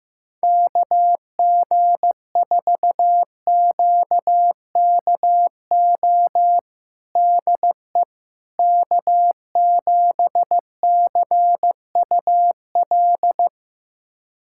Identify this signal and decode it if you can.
Morse code, heard as a keyed tone, about 15 wpm, 700 Hz. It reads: KG4QKO DE K7CUL
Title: USA Callsign DE Callsign 15wpm